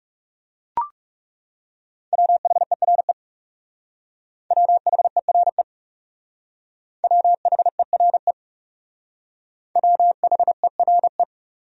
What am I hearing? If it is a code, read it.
Morse code, heard as a keyed tone, about 38 wpm, 700 Hz. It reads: WHERE WHERE WHERE WHERE
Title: Speed-Racing Top 500 Words 30wpm